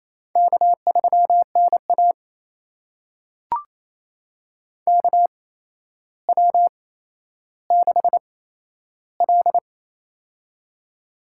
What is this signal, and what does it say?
Morse code, heard as a keyed tone, about 27 wpm, spaced as for 12 wpm, 700 Hz. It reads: K3NA KW6L
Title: USA Call Signs - 4 Characters - 8x Character Spacing - 28wpm